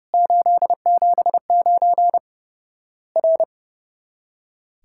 Morse code, heard as a keyed tone, about 30 wpm, 700 Hz. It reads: 879 R
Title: Worked All Europe DX Contest - QTC Exchange - 30wpm